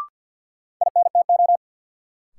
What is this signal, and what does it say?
Morse code, heard as a keyed tone, about 50 wpm, 700 Hz. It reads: INTO
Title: Sentences from Top 600 Words - New Word Drill - 50wpm